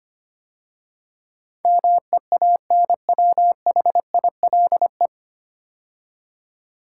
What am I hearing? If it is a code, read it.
Morse code, heard as a keyed tone, about 25 wpm, 690 Hz. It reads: MEANWHILE
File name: practice / 25wpm / 9 Letter Words - First 3 Letters Given - 25wpm